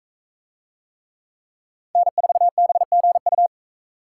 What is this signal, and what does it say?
Morse code, heard as a keyed tone, about 42 wpm, 700 Hz. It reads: N4BGU